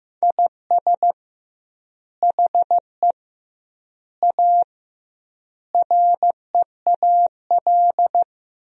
Morse code, heard as a keyed tone, about 15 wpm, 700 Hz. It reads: IS HE A REAL
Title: 2x Word-Spacing - Sentences from Top 300 Words - 15wpm